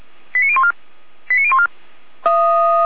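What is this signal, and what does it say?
Five-tone selective call (ZVEI), heard as quick runs of five tones, tones of about 70 ms. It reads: 89014 89014
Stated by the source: ZVEI five-tone selective call received by SDR.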